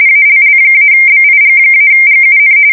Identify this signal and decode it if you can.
RTTY, about 45.45 baud. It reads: ALL-BIG-GUN
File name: fsk170 / battleshipnj